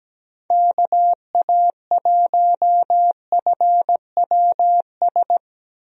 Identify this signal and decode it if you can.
Morse code, heard as a keyed tone, about 17 wpm, 700 Hz. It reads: KA1FWS
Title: USA Call Signs - 6 Characters - Repeated 2x - 17wpm